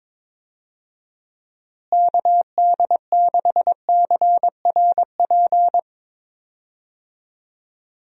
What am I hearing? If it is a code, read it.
Morse code, heard as a keyed tone, about 22 wpm, 700 Hz. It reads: KD6CRP
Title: USA Call Signs - 6 Characters - Repeated 3x - 22wpm